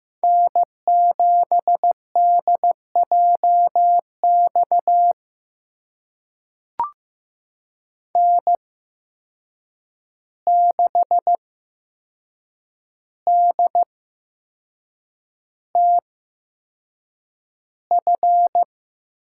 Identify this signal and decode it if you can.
Morse code, heard as a keyed tone, about 15 wpm, spaced as for 6 wpm, 700 Hz. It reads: N7DJX N6DTF
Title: USA Call Signs - 8x Character Spacing - 15wpm